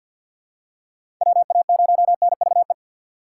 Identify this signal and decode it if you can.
Morse code, heard as a keyed tone, about 50 wpm, 700 Hz. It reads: WA0NVE
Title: USA Call Signs - 4x Character Spacing - 50wpm